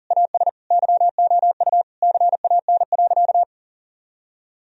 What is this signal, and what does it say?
Morse code, heard as a keyed tone, about 40 wpm, 700 Hz. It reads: AS YOU CAN.
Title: Sentences from Top 1000 Words - Review Part 1 - 40wpm